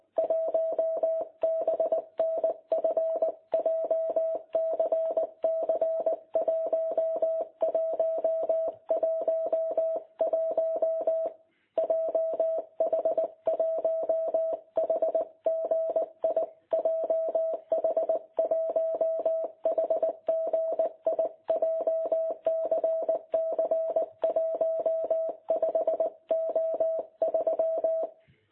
Morse code, heard as a keyed tone, about 19 wpm, 660 Hz. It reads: 1BNF JCC 1111 JH1HGI JH1HGI JCC 15O3
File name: jh8cbh / 2021cw